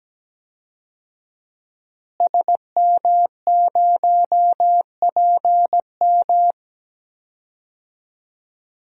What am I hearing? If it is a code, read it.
Morse code, heard as a keyed tone, about 17 wpm, 700 Hz. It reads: SM0PM
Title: DXCC-100 Call Signs - 5 Characters - Repeated 2x - 17wpm